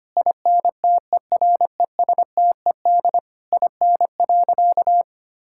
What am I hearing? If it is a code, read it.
Morse code, heard as a keyed tone, about 25 wpm, 700 Hz. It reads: INTERESTED IN.
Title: Sentences from Top 1000 Words - Review Part 6 - 25wpm